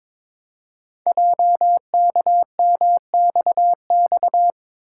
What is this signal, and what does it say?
Morse code, heard as a keyed tone, about 22 wpm, 700 Hz. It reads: JKMXX
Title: Code Groups - 5 Letters - 22wpm